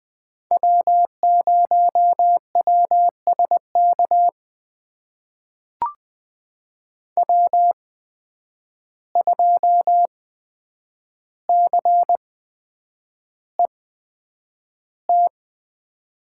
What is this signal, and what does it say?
Morse code, heard as a keyed tone, about 20 wpm, spaced as for 9 wpm, 700 Hz. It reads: W0WSK W2CET
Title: USA Call Signs - 5 Characters - 8x Character Spacing - 20wpm